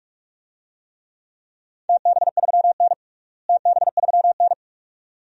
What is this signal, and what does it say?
Morse code, heard as a keyed tone, about 43 wpm, 700 Hz. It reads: TB3N TB3N
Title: DXCC-100 Call Signs - 4 Characters - Repeated 2x - 45wpm